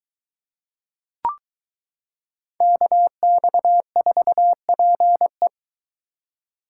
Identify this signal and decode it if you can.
Morse code, heard as a keyed tone, about 23 wpm, 700 Hz. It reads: KX4PE